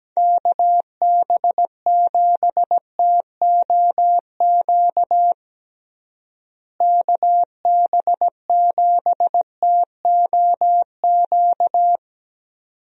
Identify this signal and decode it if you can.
Morse code, heard as a keyed tone, about 17 wpm, 700 Hz. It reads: KB7TOQ KB7TOQ
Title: USA Call Signs - 6 Characters - Repeated 2x - 17wpm